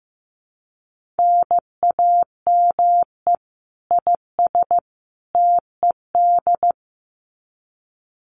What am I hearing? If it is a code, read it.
Morse code, heard as a keyed tone, about 15 wpm, 700 Hz. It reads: NAME IS TED
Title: Common QSO Phrases 15wpm